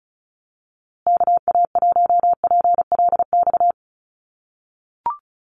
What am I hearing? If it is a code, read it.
Morse code, heard as a keyed tone, about 34 wpm, 700 Hz. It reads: KA1PLX
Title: USA Call Signs Repeated 3x 35wpm